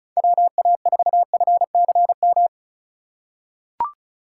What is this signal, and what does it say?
Morse code, heard as a keyed tone, about 36 wpm, 700 Hz. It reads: WA4FCM
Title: USA Call Signs - 6 Characters - 3x Character Spacing - 35wpm